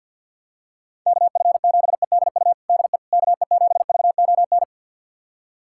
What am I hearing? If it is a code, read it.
Morse code, heard as a keyed tone, about 50 wpm, 700 Hz. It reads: KF7EDU DE KE7VON
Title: USA Callsign DE Callsign - 6 Characters - 50wpm